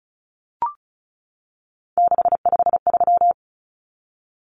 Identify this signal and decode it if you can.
Morse code, heard as a keyed tone, about 34 wpm, 700 Hz. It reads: 653
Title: Numbers - 3-Digit 35wpm